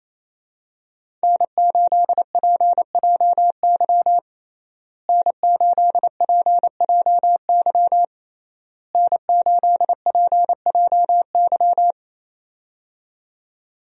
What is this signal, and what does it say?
Morse code, heard as a keyed tone, about 27 wpm, 700 Hz. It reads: N8PJY N8PJY N8PJY
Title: USA Call Signs - 5 Characters - Repeated 3x - 28wpm